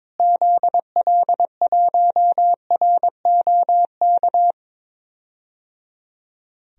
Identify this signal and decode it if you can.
Morse code, heard as a keyed tone, about 22 wpm, 700 Hz. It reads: ZL1ROK
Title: DXCC-100 Call Signs - 6 Characters - Repeated 3x - 22wpm